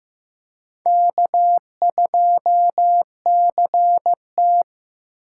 Morse code, heard as a keyed tone, about 15 wpm, 700 Hz. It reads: K2CT
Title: USA Call Signs - 4 Characters - 6x Character Spacing - 15wpm